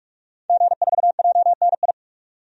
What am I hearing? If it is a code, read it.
Morse code, heard as a keyed tone, about 46 wpm, 700 Hz. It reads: G4JNI
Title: DXCC-100 Call Signs - 5 Characters - 45wpm